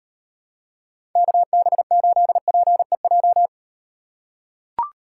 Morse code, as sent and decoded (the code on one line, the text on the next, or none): -.- -... ---.. .--. .---
KB8PEJ